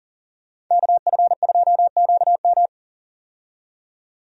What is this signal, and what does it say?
Morse code, heard as a keyed tone, about 40 wpm, 700 Hz. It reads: KF2QM
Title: USA Call Signs - 5 Characters - 40wpm